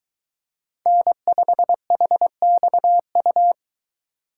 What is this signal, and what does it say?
Morse code, heard as a keyed tone, about 22 wpm, 700 Hz. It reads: N5HXU